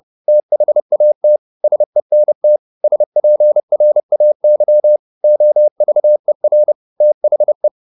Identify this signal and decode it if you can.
Morse code, heard as a keyed tone, about 30 wpm, 600 Hz. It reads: THAT SENT SPRAY OVER THE